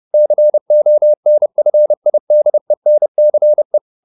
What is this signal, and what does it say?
Morse code, heard as a keyed tone, about 30 wpm, 600 Hz. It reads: CONFIDENCE